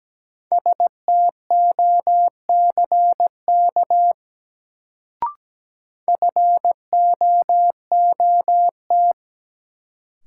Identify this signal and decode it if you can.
Morse code, heard as a keyed tone, about 17 wpm, 700 Hz. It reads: STOCK FOOT
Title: Sets of 5 Words - New Word Drill - 17wpm